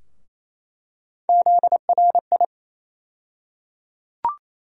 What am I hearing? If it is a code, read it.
Morse code, heard as a keyed tone, about 28 wpm, 700 Hz. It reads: ZRI
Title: USA Call Sign Postfixes - 3 Letters 28wpm